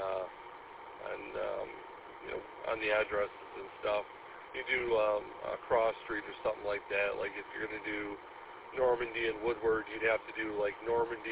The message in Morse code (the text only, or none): none